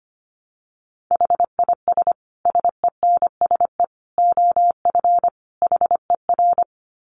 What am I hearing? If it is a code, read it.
Morse code, heard as a keyed tone, about 25 wpm, 700 Hz. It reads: HIS SENSE OF HER
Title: Dit Challenge 25wpm